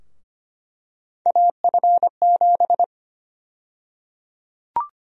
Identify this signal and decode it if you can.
Morse code, heard as a keyed tone, about 25 wpm, 700 Hz. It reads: AF7
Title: USA Call Sign Prefixes - Repeated 2x - 25wpm